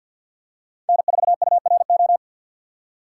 Morse code, heard as a keyed tone, about 46 wpm, 700 Hz. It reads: N4URO
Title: USA Call Signs - 5 Characters - 3x Character Spacing - 50wpm